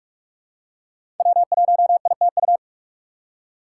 Morse code, heard as a keyed tone, about 43 wpm, 700 Hz. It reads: W1ITU
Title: USA Call Signs - 5 Characters - 4x Character Spacing - 45wpm